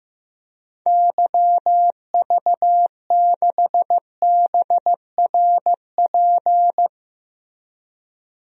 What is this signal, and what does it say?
Morse code, heard as a keyed tone, about 16 wpm, 700 Hz. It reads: YV6BRP
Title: DXCC-100 Call Signs - 6 Characters - Repeated 2x - 15wpm